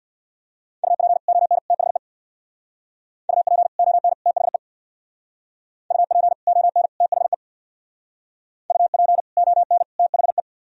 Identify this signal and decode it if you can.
Morse code, heard as a keyed tone, about 63 wpm, 700 Hz. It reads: UP ON THE UP ON THE UP ON THE UP ON THE